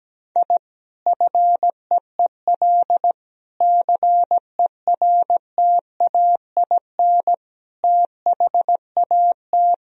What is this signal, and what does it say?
Morse code, heard as a keyed tone, about 17 wpm, 700 Hz. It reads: I FEEL CERTAIN THAT
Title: Sentences from Top 1000 Words - Review Part 2 - 17wpm